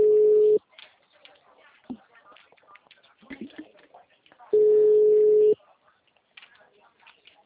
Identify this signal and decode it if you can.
Ringback tone (Europe, 425 Hz), with speech.